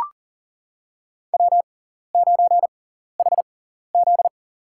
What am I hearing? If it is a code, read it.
Morse code, heard as a keyed tone, about 43 wpm, spaced as for 14 wpm, 700 Hz. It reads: W9HZ
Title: USA Call Signs - 4 Characters - 6x Character Spacing - 40wpm